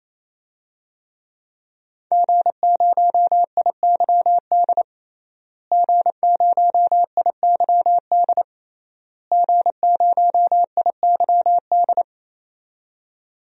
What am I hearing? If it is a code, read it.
Morse code, heard as a keyed tone, about 28 wpm, 700 Hz. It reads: G0IYD G0IYD G0IYD
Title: DXCC-100 Call Signs - 5 Characters - Repeated 3x - 28wpm